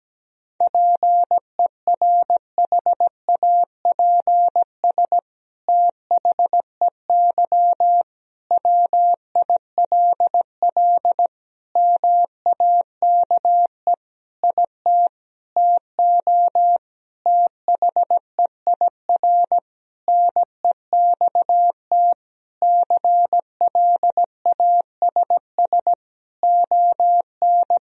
Morse code, heard as a keyed tone, about 17 wpm, 700 Hz. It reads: PERHAPS THEY WILL MAKE IT TO THEIR NEXT CLASS ON
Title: Sentences from Top 1000 Words - Review Part 2 - 17wpm